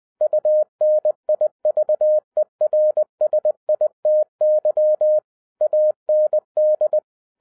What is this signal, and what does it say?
Morse code, heard as a keyed tone, about 20 wpm, 610 Hz. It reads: UNIVERSITY AND
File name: 2011-10-mp3 / n03-TARO JR0QWW 20wpm